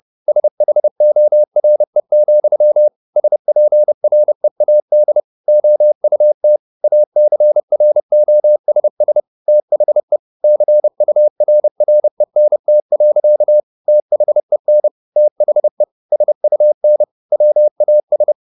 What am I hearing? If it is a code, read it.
Morse code, heard as a keyed tone, about 30 wpm, 600 Hz. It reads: SHORE, SPREAD OUT ACROSS THE CURRENT. THEN THE SUN WAS
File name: the old man and the sea / the old man and the sea 1 30wpm